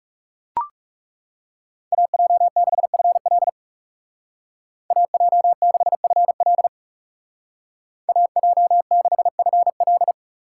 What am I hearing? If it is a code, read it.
Morse code, heard as a keyed tone, about 36 wpm, 700 Hz. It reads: AJ6FL AJ6FL AJ6FL
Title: Speed Racing - USA 2x2 Call Signs 30wpm